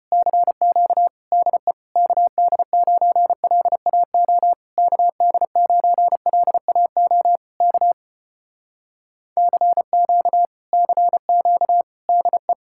Morse code, heard as a keyed tone, about 33 wpm, 700 Hz. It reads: CQ DE KD9LAO KD9LAO K CQ CQ DE